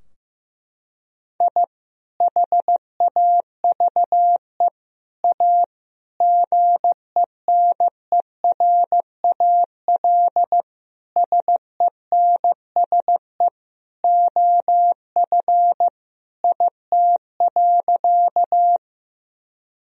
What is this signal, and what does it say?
Morse code, heard as a keyed tone, about 15 wpm, 700 Hz. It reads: I HAVE A GENERAL SENSE OF IT.
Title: Sentences from Top 1000 Words - Review Part 4 - 15wpm